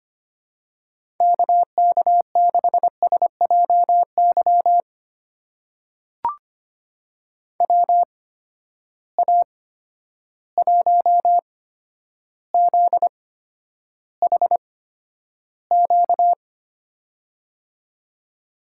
Morse code, heard as a keyed tone, about 25 wpm, spaced as for 11 wpm, 700 Hz. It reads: KK6SJY WA1ZHQ
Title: USA Call Signs - 6 Characters - 8x Character Spacing - 25wpm